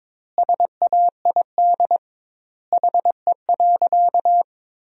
Morse code, heard as a keyed tone, about 21 wpm, 700 Hz. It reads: SAID HE.
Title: 2x Word-Spacing - Sentences from Top 100 Words - 22wpm